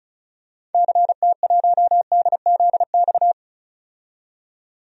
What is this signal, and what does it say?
Morse code, heard as a keyed tone, about 36 wpm, 700 Hz. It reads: CT1DZX
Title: DXCC-100 Call Signs - 6 Characters - 35wpm